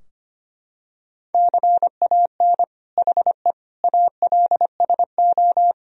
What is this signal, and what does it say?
Morse code, heard as a keyed tone, about 25 wpm, 700 Hz. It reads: CAN HE ALSO
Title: Sentences from Top 1000 Words - Review Part 3 - 25wpm